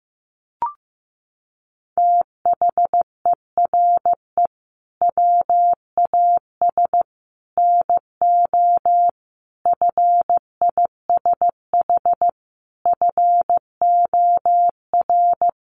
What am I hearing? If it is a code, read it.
Morse code, heard as a keyed tone, about 15 wpm, 700 Hz. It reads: THERE WAS NO FISH FOR
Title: Sentences from Top 1500 Words 15wpm